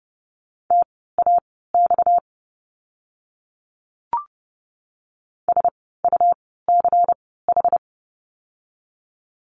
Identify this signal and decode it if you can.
Morse code, heard as a keyed tone, about 30 wpm, spaced as for 17 wpm, 700 Hz. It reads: TAX SUCH
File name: course / 30:10wpm / 230 Lesson 37f - Letter x - Words 30wpm